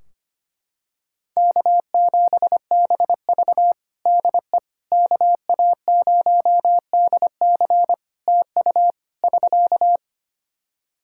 Morse code, heard as a keyed tone, about 25 wpm, 700 Hz. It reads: K7BV DE KA0DC TU <SK>
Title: Practice QSO - Exchange 4 - Ending - 25wpm